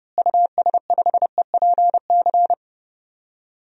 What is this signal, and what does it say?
Morse code, heard as a keyed tone, about 32 wpm, 700 Hz. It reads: US5EPC
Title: DXCC-100 Call Signs - 6 Characters - Repeated 3x - 30wpm